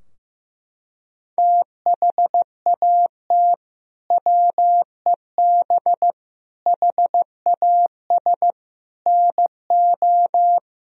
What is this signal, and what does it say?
Morse code, heard as a keyed tone, about 15 wpm, 700 Hz. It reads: THAT WEB HAS NO
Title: Sentences from Top 1000 Words - Review Part 4 - 15wpm